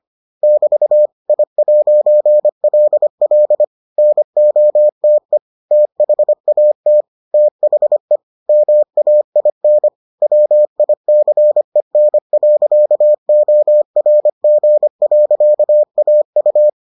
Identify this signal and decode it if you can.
Morse code, heard as a keyed tone, about 25 wpm, 600 Hz. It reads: <BT> I'LL NOTE THAT THE MAIN WICEN.ORG.AU